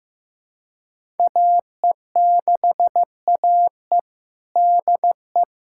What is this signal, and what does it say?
Morse code, heard as a keyed tone, about 15 wpm, 700 Hz. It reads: AE6AE DE
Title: USA Callsign DE Callsign - 5 Characters - 15wpm